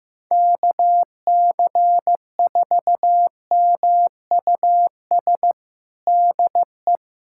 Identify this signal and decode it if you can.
Morse code, heard as a keyed tone, about 15 wpm, 700 Hz. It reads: KC4MUS DE
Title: USA Callsign DE Callsign 15wpm